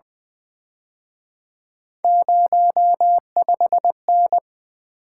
Morse code, heard as a keyed tone, about 24 wpm, 700 Hz. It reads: 05N
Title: Speed-Racing - Three Letter-Number Combinations - 20wpm